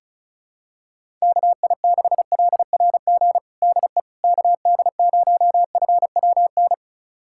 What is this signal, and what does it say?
Morse code, heard as a keyed tone, about 36 wpm, 700 Hz. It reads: KI6LRG DE KD0FWN
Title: USA Callsign DE Callsign - 6 Characters - 35wpm